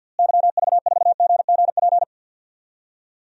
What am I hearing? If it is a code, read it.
Morse code, heard as a keyed tone, about 53 wpm, 700 Hz. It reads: YV4GGP
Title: DXCC-100 Call Signs - 6 Characters - Repeated 2x - 50wpm